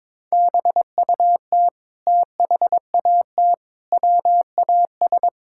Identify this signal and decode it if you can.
Morse code, heard as a keyed tone, about 22 wpm, 700 Hz. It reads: BUT THAT WAS
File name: practice / 22wpm / Sentences with 8-Word Phrases - 22wpm